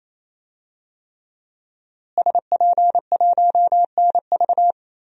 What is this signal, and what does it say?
Morse code, heard as a keyed tone, about 28 wpm, 700 Hz. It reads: SP1NV
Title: DXCC-100 Call Signs - 5 Characters - Repeated 3x - 28wpm